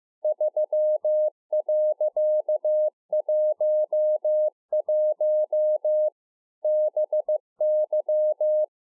Morse code, heard as a keyed tone, about 15 wpm, 610 Hz. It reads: 3.11 BY